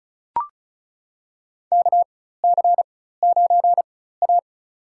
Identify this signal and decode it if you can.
Morse code, heard as a keyed tone, about 34 wpm, spaced as for 16 wpm, 700 Hz. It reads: KC9A
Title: USA Call Signs - 4 Characters - 4x Character Spacing - 35wpm